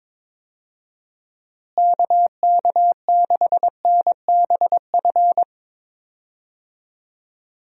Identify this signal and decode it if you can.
Morse code, heard as a keyed tone, about 22 wpm, 700 Hz. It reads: KK6NBF
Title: USA Call Signs - 6 Characters - Repeated 2x - 22wpm